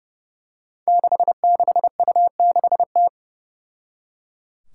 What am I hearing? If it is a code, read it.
Morse code, heard as a keyed tone, about 30 wpm, 700 Hz. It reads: 66U6T
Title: Five Challenging Letter-Number Combinations 30wpm